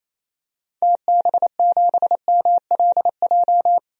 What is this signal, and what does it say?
Morse code, heard as a keyed tone, about 28 wpm, 700 Hz. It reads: TB7MLJ
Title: DXCC-100 Call Signs - 6 Characters - 28wpm